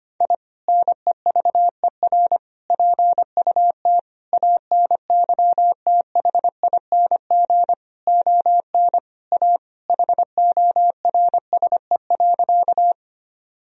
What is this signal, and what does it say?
Morse code, heard as a keyed tone, about 25 wpm, 700 Hz. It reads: I NEVER PUT ANYTHING ON A HORSE.
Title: Sentences from Top 1000 Words - Review Part 1 - 25wpm